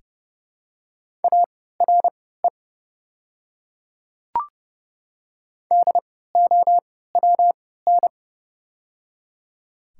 Morse code, heard as a keyed tone, about 30 wpm, spaced as for 17 wpm, 700 Hz. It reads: ARE DOWN
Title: 3x Character Spacing - Top 100 Words - 30wpm